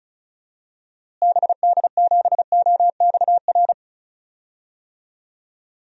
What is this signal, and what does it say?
Morse code, heard as a keyed tone, about 36 wpm, 700 Hz. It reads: BD7OXR
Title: DXCC-100 Call Signs - 6 Characters - Repeated 2x - 35wpm